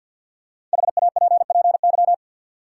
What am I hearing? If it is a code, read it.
Morse code, heard as a keyed tone, about 52 wpm, 700 Hz. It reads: HAPPY
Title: Sentences from Top 1000 Words - New Word Drill - 50wpm